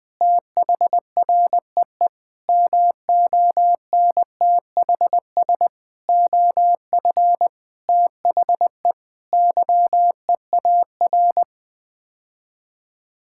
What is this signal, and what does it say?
Morse code, heard as a keyed tone, about 19 wpm, 700 Hz. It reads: THREE MONTHS OF THE YEAR
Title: Sets of 8 Words - 20wpm